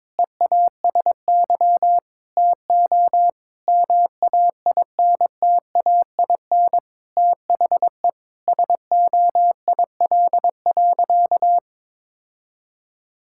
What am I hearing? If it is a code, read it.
Morse code, heard as a keyed tone, about 22 wpm, 700 Hz. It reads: EASY TO MAINTAIN THE SOIL.
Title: Sentences from Top 1000 Words - Review Part 5 - 22wpm